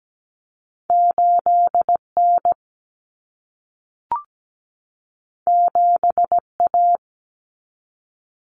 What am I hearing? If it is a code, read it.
Morse code, heard as a keyed tone, about 17 wpm, 700 Hz. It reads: 8N 7A
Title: Common Cut Numbers - 2-Digit 17wpm